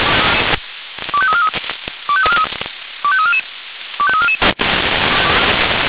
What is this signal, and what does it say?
Five-tone selective call (ZVEI), heard as quick runs of five tones, tones of about 70 ms. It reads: 25343 25344 25344
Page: ZVEI